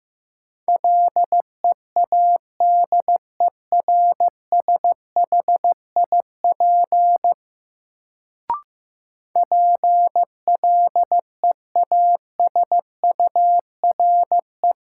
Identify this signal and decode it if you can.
Morse code, heard as a keyed tone, about 15 wpm, 700 Hz. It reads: LEADERSHIP PLEASURE
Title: Sentences from Top 1600 Words - New Word Drill - 15wpm